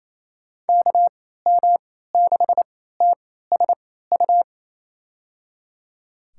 Morse code, heard as a keyed tone, about 28 wpm, spaced as for 16 wpm, 700 Hz. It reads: KM6TSU
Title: USA Call Signs - 6 Characters - 3x Character Spacing - 28wpm